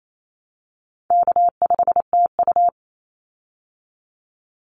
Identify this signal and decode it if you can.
Morse code, heard as a keyed tone, about 28 wpm, 700 Hz. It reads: K5TU
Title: Call Signs Repeated 3x - 28wpm